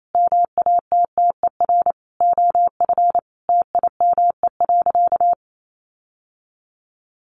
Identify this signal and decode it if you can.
Morse code, heard as a keyed tone, about 28 wpm, 700 Hz. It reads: MATTER OF TIME.
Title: Sentences from Top 1400 Words 28wpm